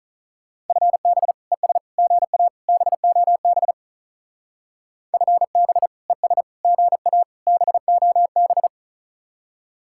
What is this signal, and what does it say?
Morse code, heard as a keyed tone, about 45 wpm, 700 Hz. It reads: FB ES GA BOB FB ES GA BOB